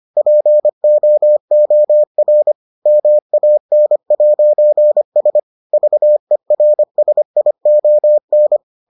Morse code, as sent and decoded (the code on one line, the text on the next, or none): .--. --- --- .-. / -- .- -. .----. ... / ...- . .-. ... .. --- -.
POOR MAN'S VERSION